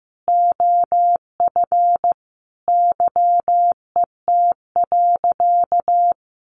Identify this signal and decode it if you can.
Morse code, heard as a keyed tone, about 15 wpm, 700 Hz. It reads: OF YET.
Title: A Sound Of Thunder 15wpm